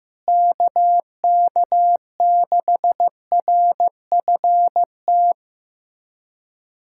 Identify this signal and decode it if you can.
Morse code, heard as a keyed tone, about 15 wpm, 700 Hz. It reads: KK6RFT
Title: USA Callsign DE Callsign - 6 Characters - 15wpm